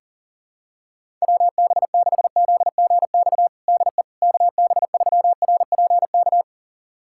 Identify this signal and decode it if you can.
Morse code, heard as a keyed tone, about 40 wpm, 700 Hz. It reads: WB6ZGX DE KB3RPK
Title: USA Callsign DE Callsign 40wpm